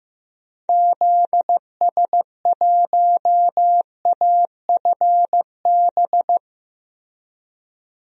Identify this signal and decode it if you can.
Morse code, heard as a keyed tone, about 15 wpm, 700 Hz. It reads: ZS1AFB
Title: DXCC-100 Call Signs - 6 Characters - Repeated 3x - 15wpm